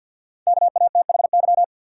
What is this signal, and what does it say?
Morse code, heard as a keyed tone, about 52 wpm, 700 Hz. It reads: KATHY
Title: USA Common Names - 50wpm